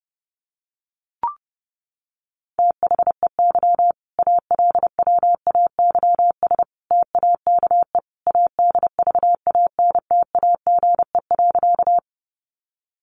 Morse code, heard as a keyed tone, about 30 wpm, 700 Hz. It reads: THEY ALWAYS TAKE ADVANTAGE.
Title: Sentences from Top 900 Words 30wpm